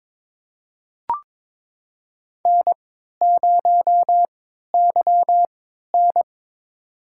Morse code, heard as a keyed tone, about 22 wpm, spaced as for 13 wpm, 700 Hz. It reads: N0YN
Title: USA Call Signs - 3x Character Spacing - 22wpm